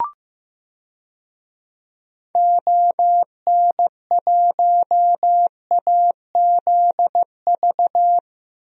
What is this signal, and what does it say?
Morse code, heard as a keyed tone, about 15 wpm, 700 Hz. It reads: ON1AZV
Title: DXCC-100 Call Signs - 6 Characters - 15wpm